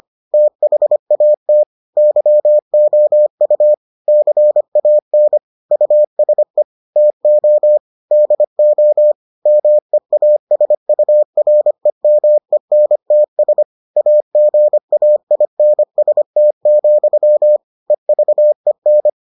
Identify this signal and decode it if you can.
Morse code, heard as a keyed tone, about 25 wpm, 600 Hz. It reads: THAT YOU CAN USE TO DO MEASUREMENTS AGAINST, EVEN